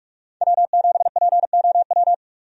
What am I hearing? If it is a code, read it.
Morse code, heard as a keyed tone, about 43 wpm, 700 Hz. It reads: W7POW
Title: USA Call Signs - 5 Characters - 45wpm